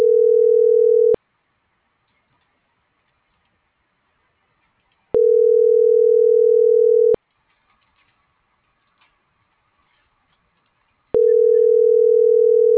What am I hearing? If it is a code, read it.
Ringback tone (North America), 440 + 480 Hz.